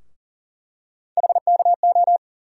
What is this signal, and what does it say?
Morse code, heard as a keyed tone, about 40 wpm, 700 Hz. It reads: HKO
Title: USA Call Sign Postfixes - 3 Letters 40wpm